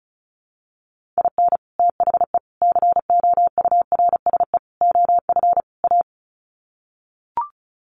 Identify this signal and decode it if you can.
Morse code, heard as a keyed tone, about 34 wpm, 700 Hz. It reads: IN THE COURSE OF A
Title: Sets of 5 Words Encore2 35wpm